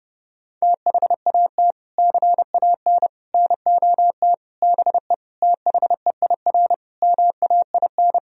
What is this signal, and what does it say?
Morse code, heard as a keyed tone, about 32 wpm, 700 Hz. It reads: THAT CAN NOT BE THEIR MAIN
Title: Sentences from Top 1000 Words - Review Part 2 - 30wpm